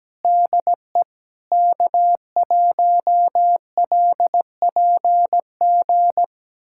Morse code, heard as a keyed tone, about 17 wpm, 700 Hz. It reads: DE K1LPG
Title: USA Callsign DE Callsign - 5 Characters - 17wpm